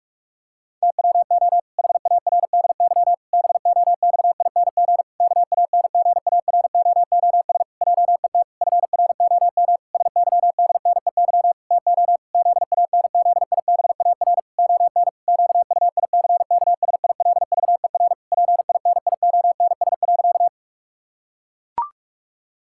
Morse code, heard as a keyed tone, about 41 wpm, 700 Hz. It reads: TWO HARDY BOXING KANGAROOS JET FROM SYDNEY TO ZANZIBAR ON QUICKSILVER PINIONS.
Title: Pangram Challenge v2 - 45wpm